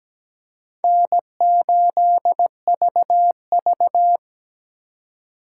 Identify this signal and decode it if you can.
Morse code, heard as a keyed tone, about 20 wpm, 700 Hz. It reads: N8VV
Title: Speed Racing - USA 1x2 Call Signs 17wpm